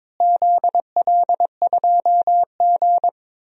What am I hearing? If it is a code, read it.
Morse code, heard as a keyed tone, about 22 wpm, 700 Hz. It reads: ZL2G
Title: DXCC-100 Call Signs - 4 Characters - Repeated 3x - 22wpm